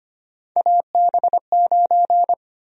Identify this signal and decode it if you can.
Morse code, heard as a keyed tone, about 25 wpm, 700 Hz. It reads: AB9
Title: USA Call Sign Prefixes - 25wpm